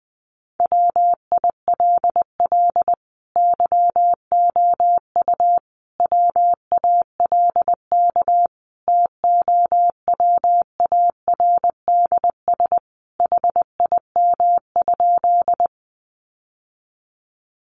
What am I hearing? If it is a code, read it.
Morse code, heard as a keyed tone, about 20 wpm, 700 Hz. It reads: WILL YOU WALK TOWARDS HIM?
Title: Sentences from Top 1100 Words 20wpm